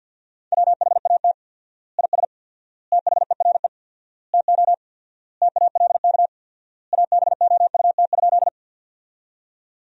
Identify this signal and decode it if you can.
Morse code, heard as a keyed tone, about 50 wpm, 700 Hz. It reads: WHAT IS THERE TO TALK ABOUT?
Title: 4x Word-Spacing - Sentences from Top 400 Words - 50wpm